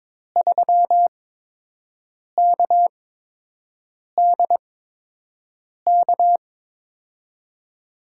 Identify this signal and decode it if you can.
Morse code, heard as a keyed tone, about 22 wpm, spaced as for 6 wpm, 700 Hz. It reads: 3KDK
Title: USA Call Signs - 5 Characters - 8x Character Spacing - 22wpm